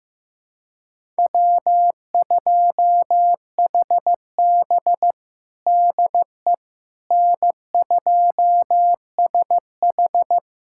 Morse code, heard as a keyed tone, about 15 wpm, 700 Hz. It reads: W2HB DE N2SH
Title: USA Callsign DE Callsign - 4 Characters - 15wpm